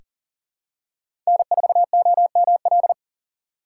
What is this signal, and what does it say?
Morse code, heard as a keyed tone, about 40 wpm, 700 Hz. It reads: N4OML
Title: USA Call Signs - 4x Character Spacing - 40wpm